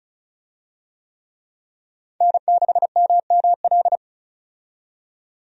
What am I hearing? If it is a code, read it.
Morse code, heard as a keyed tone, about 36 wpm, 700 Hz. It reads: N6MML
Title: USA Call Signs - 5 Characters - 35wpm